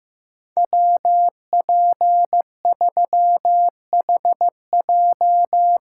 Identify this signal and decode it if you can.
Morse code, heard as a keyed tone, about 16 wpm, 700 Hz. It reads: WP3HJ
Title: Speed Racing - USA 2x2 Call Signs 15wpm